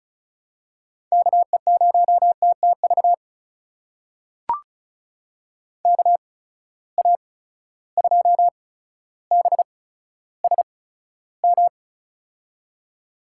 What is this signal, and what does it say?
Morse code, heard as a keyed tone, about 36 wpm, spaced as for 15 wpm, 700 Hz. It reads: KE0TTV KA2BSM
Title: USA Call Signs - 6 Characters - 8x Character Spacing - 35wpm